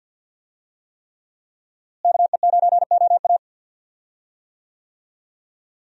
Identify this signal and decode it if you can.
Morse code, heard as a keyed tone, about 52 wpm, 700 Hz. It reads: KE9OA
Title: USA Call Signs - 5 Characters - Repeated 3x - 50wpm